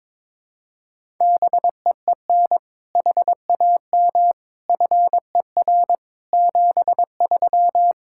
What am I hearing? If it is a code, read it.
Morse code, heard as a keyed tone, about 22 wpm, 700 Hz. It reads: BEEN HAM FER 73
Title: K1USN SST QSO Practice 22wpm